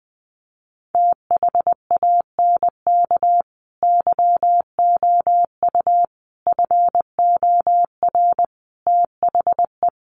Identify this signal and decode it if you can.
Morse code, heard as a keyed tone, about 20 wpm, 700 Hz. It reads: THANK YOU FOR THE
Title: Sentences from Top 600 Words 20wpm